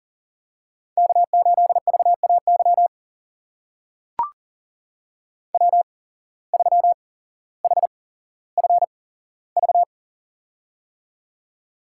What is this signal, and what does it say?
Morse code, heard as a keyed tone, about 37 wpm, spaced as for 17 wpm, 700 Hz. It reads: K8VAY W3HFV
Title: USA Call Signs - 5 Characters - 8x Character Spacing - 40wpm